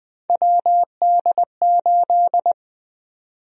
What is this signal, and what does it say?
Morse code, heard as a keyed tone, about 21 wpm, 700 Hz. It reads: WD8
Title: USA Call Sign Prefixes - Repeated 2x - 20wpm